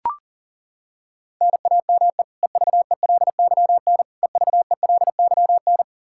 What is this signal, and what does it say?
Morse code, heard as a keyed tone, about 37 wpm, 700 Hz. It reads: NAME EVELYN EVELYN
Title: Practice QSO - Exchange 1 - Beginning - 40wpm